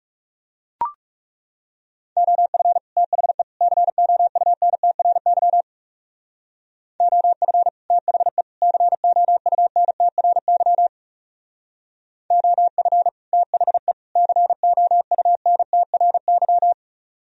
Morse code, heard as a keyed tone, about 36 wpm, 700 Hz. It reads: OF THE COUNTRY OF THE COUNTRY OF THE COUNTRY
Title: Speed-Racing - Sets of 3 Words 30wpm